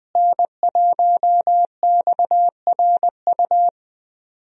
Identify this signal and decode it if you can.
Morse code, heard as a keyed tone, about 20 wpm, 700 Hz. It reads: N1XRU
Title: USA Call Signs - 5 Characters - Repeated 2x - 20wpm